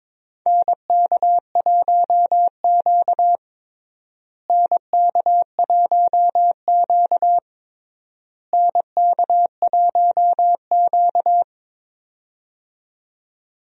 Morse code, heard as a keyed tone, about 22 wpm, 700 Hz. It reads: NK1Q NK1Q NK1Q
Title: USA Call Signs - 4 Characters - Repeated 3x - 22wpm